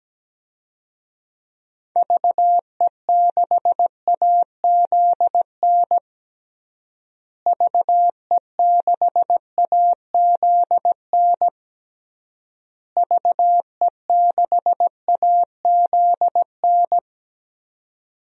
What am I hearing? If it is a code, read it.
Morse code, heard as a keyed tone, about 17 wpm, 700 Hz. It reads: VE6AZN VE6AZN VE6AZN
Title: DXCC-100 Call Signs - 6 Characters - Repeated 3x - 17wpm